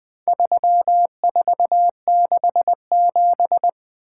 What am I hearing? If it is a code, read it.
Morse code, heard as a keyed tone, about 22 wpm, 700 Hz. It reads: 3467
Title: Speed-Racing - Four Letter-Number Combinations - 20wpm